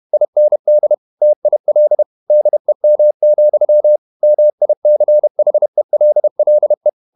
Morse code, heard as a keyed tone, about 31 wpm, 600 Hz. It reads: IND TIL DEM, MICHELLE
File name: Ferie QRQ 31wpm